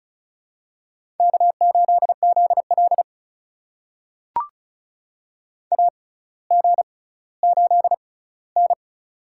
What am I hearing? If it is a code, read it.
Morse code, heard as a keyed tone, about 36 wpm, spaced as for 18 wpm, 700 Hz. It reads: K8ZL AG8N
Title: USA Call Signs - 4 Characters - 6x Character Spacing - 35wpm